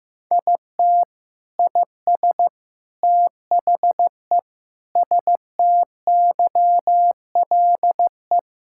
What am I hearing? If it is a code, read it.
Morse code, heard as a keyed tone, about 16 wpm, 700 Hz. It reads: IT IS THE STYLE
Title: Sentences from Top 1000 Words - Review Part 5 - 15wpm